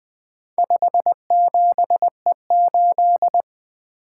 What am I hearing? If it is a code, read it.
Morse code, heard as a keyed tone, about 23 wpm, 700 Hz. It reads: 57E8
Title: Speed-Racing - Four Letter-Number Combinations - 20wpm